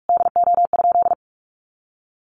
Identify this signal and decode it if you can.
Morse code, heard as a keyed tone, about 44 wpm, 700 Hz. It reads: DO?
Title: Sentences from Top 200 Words 45wpm